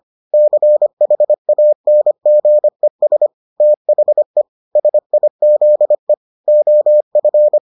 Morse code, heard as a keyed tone, about 25 wpm, 600 Hz. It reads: CHANGES THE SIZE OF